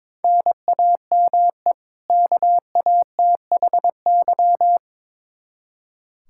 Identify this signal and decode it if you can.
Morse code, heard as a keyed tone, about 22 wpm, 700 Hz. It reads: NAME KATHY
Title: Practice QSO - Exchange 1 - Beginning - 22wpm